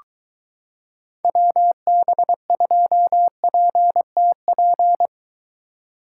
Morse code, heard as a keyed tone, about 22 wpm, 700 Hz. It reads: WB2PTP